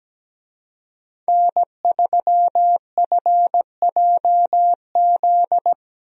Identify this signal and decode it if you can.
Morse code, heard as a keyed tone, about 17 wpm, 700 Hz. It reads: N3FJZ
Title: USA Call Signs - 5 Characters - Repeated 2x - 17wpm